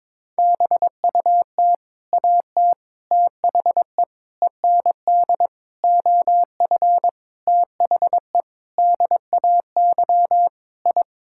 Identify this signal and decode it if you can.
Morse code, heard as a keyed tone, about 22 wpm, 700 Hz. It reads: BUT AT THE END OF THE DAY I
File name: practice / 22wpm / Sets of 8 Words - 22wpm